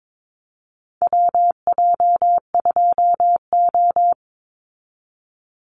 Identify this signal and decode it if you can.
Morse code, heard as a keyed tone, about 22 wpm, 700 Hz. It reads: WJ2O
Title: Call Signs Repeated 3x - 22wpm